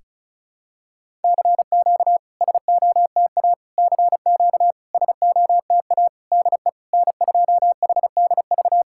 Morse code, heard as a keyed tone, about 36 wpm, 690 Hz. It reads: CQ SOTA CQ SOTA DE N2HDV
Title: SOTA QSO Practice 35wpm